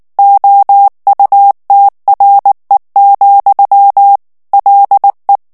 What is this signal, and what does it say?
Morse code, heard as a keyed tone, about 19 wpm, 800 Hz. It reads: OUTRE, LE
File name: FRANCAIS TEXTE CLAIR 1 VITESSE 18 MOTS MINUTES